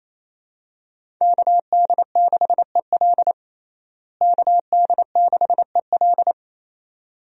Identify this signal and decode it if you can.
Morse code, heard as a keyed tone, about 28 wpm, 700 Hz. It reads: KD6EL KD6EL
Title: USA Call Signs - 5 Characters - Repeated 2x - 28wpm